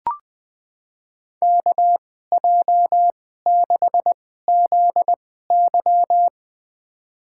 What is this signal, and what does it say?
Morse code, heard as a keyed tone, about 20 wpm, spaced as for 14 wpm, 700 Hz. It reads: KJ6ZY
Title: USA Call Signs - 5 Characters - 2x Character Spacing - 20wpm